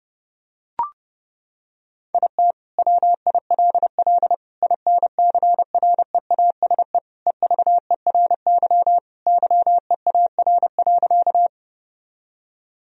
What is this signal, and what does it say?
Morse code, heard as a keyed tone, about 30 wpm, 700 Hz. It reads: IT WILL INCREASE EVERY YEAR.
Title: Sentences from Top 1000 Words - Review Part 4 - 30wpm